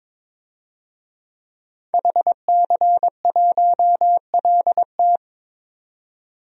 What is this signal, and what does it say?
Morse code, heard as a keyed tone, about 22 wpm, 700 Hz. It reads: HC1LT
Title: DXCC-100 Call Signs - 5 Characters - 22wpm